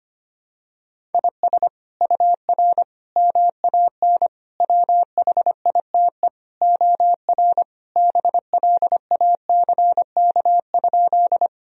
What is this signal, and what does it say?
Morse code, heard as a keyed tone, about 25 wpm, 700 Hz. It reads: IS UR MAN WHITE OR BLACK?
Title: Sentences from Top 400 Words and Abbreviations 25wpm